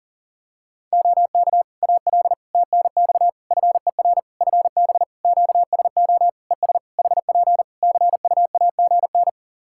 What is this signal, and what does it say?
Morse code, heard as a keyed tone, about 40 wpm, 700 Hz. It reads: OK AL TNX FER FB QSO ES HP CUAGN
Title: Practice QSO - Exchange 4 - Ending - 40wpm